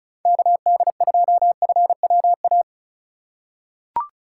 Morse code, heard as a keyed tone, about 36 wpm, 700 Hz. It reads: KD2FWA
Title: USA Call Signs - 6 Characters - 4x Character Spacing - 35wpm